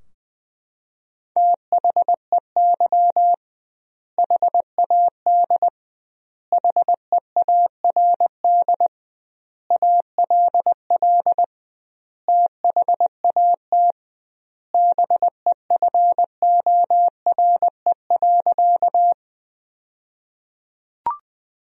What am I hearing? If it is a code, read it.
Morse code, heard as a keyed tone, about 20 wpm, 700 Hz. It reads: THEY HAD HEARD ALL THAT BEFORE.
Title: 2x Word-Spacing - Sentences from Top 400 Words - 20wpm